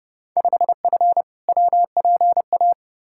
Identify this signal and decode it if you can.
Morse code, heard as a keyed tone, about 30 wpm, 700 Hz. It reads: 5F WPA
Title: Field Day Exchanges Encore - 30wpm